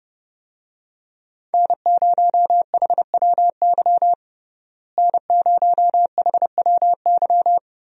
Morse code, heard as a keyed tone, about 32 wpm, 700 Hz. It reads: N0HWY N0HWY
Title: USA Call Signs - 5 Characters - Repeated 3x - 30wpm